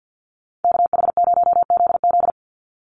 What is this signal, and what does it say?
Morse code, heard as a keyed tone, about 50 wpm, 700 Hz. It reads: KH0ZZ
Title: Call Signs - 50wpm